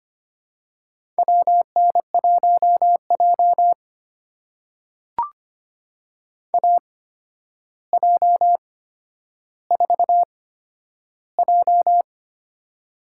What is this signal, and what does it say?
Morse code, heard as a keyed tone, about 25 wpm, spaced as for 11 wpm, 700 Hz. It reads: WN1J AJ4J
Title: USA Call Signs - 4 Characters - 8x Character Spacing - 25wpm